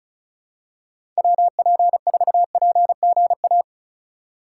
Morse code, heard as a keyed tone, about 36 wpm, 700 Hz. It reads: WP4PGA
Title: USA Call Signs - 6 Characters - 2x Character Spacing - 35wpm